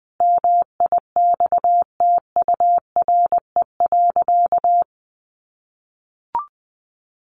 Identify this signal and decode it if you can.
Morse code, heard as a keyed tone, about 19 wpm, 700 Hz. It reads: MIXTURE.
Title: Sentences from Top 1500 Words 20wpm